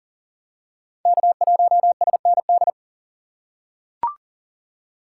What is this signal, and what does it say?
Morse code, heard as a keyed tone, about 40 wpm, 700 Hz. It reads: K1SND
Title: K1USN SST QSO Practice - 40wpm